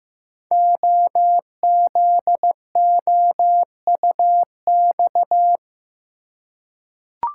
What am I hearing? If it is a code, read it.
Morse code, heard as a keyed tone, about 15 wpm, 700 Hz. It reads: OZOUX
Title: Code Groups - 5 Letter:Numbers - 15wpm